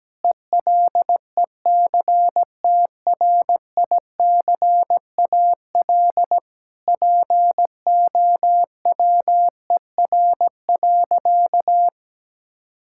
Morse code, heard as a keyed tone, about 17 wpm, 700 Hz. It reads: ELECTRICAL POWER.
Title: Sentences from Top 1000 Words - Review Part 6 - 17wpm